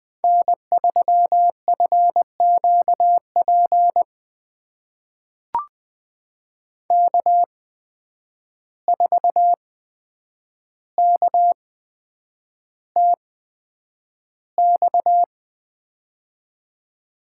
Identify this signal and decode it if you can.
Morse code, heard as a keyed tone, about 20 wpm, spaced as for 9 wpm, 700 Hz. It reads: N3FQP K4KTX
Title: USA Call Signs - 5 Characters - 8x Character Spacing - 20wpm